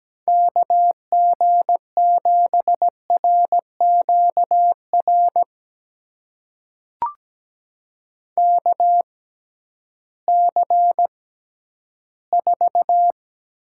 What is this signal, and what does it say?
Morse code, heard as a keyed tone, about 17 wpm, spaced as for 11 wpm, 700 Hz. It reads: KG7RQR KC4
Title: USA Call Signs - 6 Characters - 6x Character Spacing - 17wpm